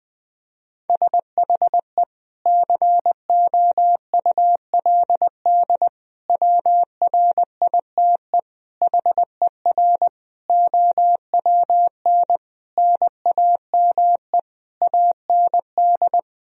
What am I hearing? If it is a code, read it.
Morse code, heard as a keyed tone, about 20 wpm, 700 Hz. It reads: SHE COULD WRITE HER OWN NAME AND
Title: Sentences from Top 1000 Words - Review Part 1 - 20wpm